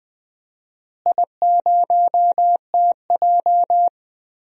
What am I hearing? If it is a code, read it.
Morse code, heard as a keyed tone, about 20 wpm, 700 Hz. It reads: I0TJ
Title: DXCC-100 Call Signs - 4 Characters - Repeated 3x - 20wpm